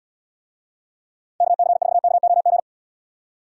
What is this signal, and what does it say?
Morse code, heard as a keyed tone, about 78 wpm, 700 Hz. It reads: KC4KOG